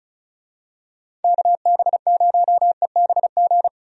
Morse code, heard as a keyed tone, about 36 wpm, 700 Hz. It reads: KB0EBG
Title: USA Call Signs - 6 Characters - Repeated 2x - 35wpm